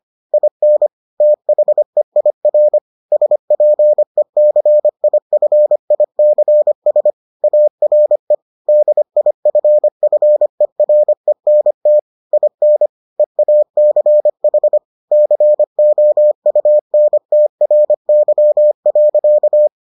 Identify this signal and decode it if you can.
Morse code, heard as a keyed tone, about 25 wpm, 600 Hz. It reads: IN THEIR SPECIFICS ARE DIFFERENT IN EACH COUNTRY.